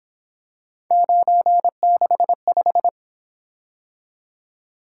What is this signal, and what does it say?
Morse code, heard as a keyed tone, about 27 wpm, 700 Hz. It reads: 965
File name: Speed-Racing - Three Letter-Number Combinations - 22wpm